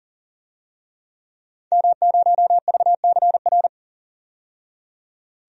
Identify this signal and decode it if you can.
Morse code, heard as a keyed tone, about 40 wpm, 700 Hz. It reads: M0VCR
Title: DXCC-100 Call Signs - 5 Characters - 40wpm